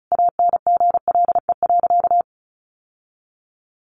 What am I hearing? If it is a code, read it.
Morse code, heard as a keyed tone, about 34 wpm, 700 Hz. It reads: ANGLE.
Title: Sentences from Top 1500 Words 35wpm